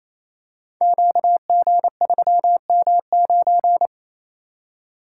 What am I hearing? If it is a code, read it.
Morse code, heard as a keyed tone, about 27 wpm, 700 Hz. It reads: QG3M9
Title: Code Groups - 5 Letter:Numbers - 28wpm